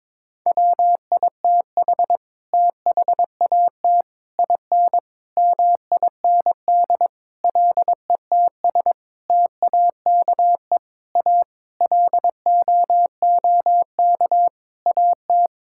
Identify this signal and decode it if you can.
Morse code, heard as a keyed tone, about 22 wpm, 700 Hz. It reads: WITH THAT IN MIND LETS TAKE A LOOK AT
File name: Sets of 9 Words - 22wpm